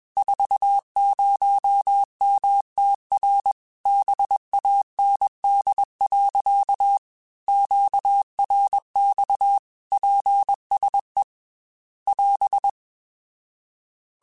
Morse code, heard as a keyed tone, about 21 wpm, 800 Hz. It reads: 40MTR BAND. QRX PSE <AS>